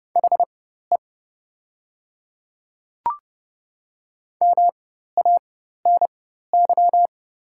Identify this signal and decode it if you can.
Morse code, heard as a keyed tone, about 30 wpm, spaced as for 14 wpm, 700 Hz. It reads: HE MANY
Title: 4x Character Spacing - Top 100 Words - 30wpm